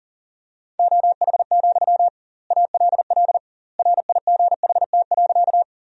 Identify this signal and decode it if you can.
Morse code, heard as a keyed tone, about 40 wpm, 700 Hz. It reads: OH, ALL RIGHT.
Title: 2x Word-Spacing - Sentences from Top 500 Words - 40wpm